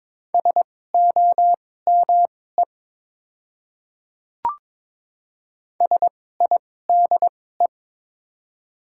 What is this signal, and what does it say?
Morse code, heard as a keyed tone, about 22 wpm, spaced as for 16 wpm, 690 Hz. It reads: SOME SIDE
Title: 2x Character Spacing - Top 100 Words - 22wpm